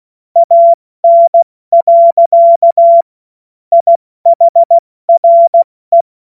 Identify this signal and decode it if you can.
Morse code, heard as a keyed tone, about 16 wpm, 680 Hz. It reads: AN. IHRE
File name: Morsekurs / Witze / Witz 11. April 70 F80